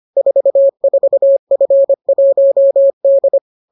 Morse code, heard as a keyed tone, about 25 wpm, 550 Hz. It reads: 44F1D